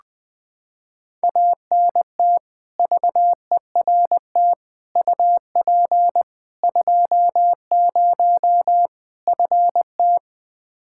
Morse code, heard as a keyed tone, about 21 wpm, 700 Hz. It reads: ANT VERT UP 20 FT
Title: Practice QSO - Exchange 2 - Info - 20wpm